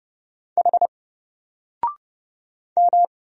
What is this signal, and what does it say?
Morse code, heard as a keyed tone, about 30 wpm, 700 Hz. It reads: H M
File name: Single-Letter-Number-30wpm